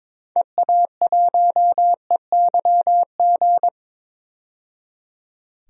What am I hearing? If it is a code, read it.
Morse code, heard as a keyed tone, about 22 wpm, 700 Hz. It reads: EA1EYG
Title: DXCC-100 Call Signs - 6 Characters - 22wpm